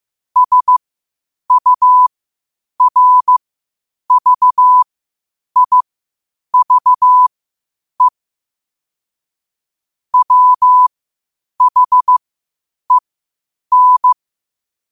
Morse code, heard as a keyed tone, about 15 wpm, spaced as for 8 wpm, 1000 Hz. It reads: SURVIVE WHEN
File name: qotdfiles / 8wpm